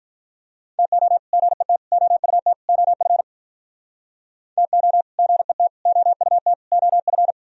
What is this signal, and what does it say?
Morse code, heard as a keyed tone, about 47 wpm, 700 Hz. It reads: TO GET OUT OF TO GET OUT OF